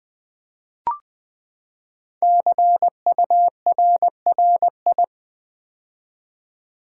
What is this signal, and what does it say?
Morse code, heard as a keyed tone, about 20 wpm, 700 Hz. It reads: CURRI
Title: Build A Word - 5 Letters At A Time - 10-Letter Words - 20wpm